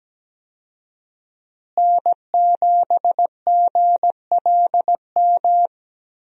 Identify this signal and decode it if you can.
Morse code, heard as a keyed tone, about 17 wpm, 700 Hz. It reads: N7GLM
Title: USA Call Signs - 5 Characters - Repeated 2x - 17wpm